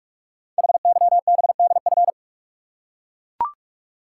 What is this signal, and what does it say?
Morse code, heard as a keyed tone, about 44 wpm, 690 Hz. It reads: HYBDF
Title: Code Groups - 5 Letter:Numbers - 45wpm